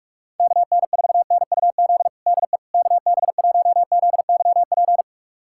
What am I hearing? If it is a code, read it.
Morse code, heard as a keyed tone, about 41 wpm, 700 Hz. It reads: KN4NUZ DE KB1ZYP
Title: USA Callsign DE Callsign 45wpm